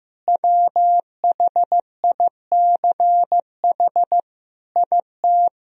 Morse code, heard as a keyed tone, about 15 wpm, 700 Hz. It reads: WHICH IT
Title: Sentences from Top 1000 Words - Review Part 2 - 15wpm